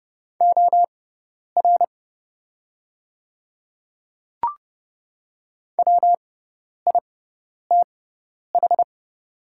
Morse code, heard as a keyed tone, about 28 wpm, spaced as for 10 wpm, 700 Hz. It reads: OR WITH
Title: 6x Character Spacing - Top 100 Words - 30wpm